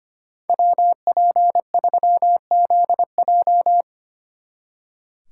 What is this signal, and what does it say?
Morse code, heard as a keyed tone, about 30 wpm, 700 Hz. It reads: WP3ZJ
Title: Speed Racing - USA 2x2 Call Signs 25wpm